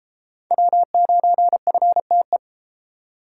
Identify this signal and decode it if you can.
Morse code, heard as a keyed tone, about 33 wpm, 700 Hz. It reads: W9FTE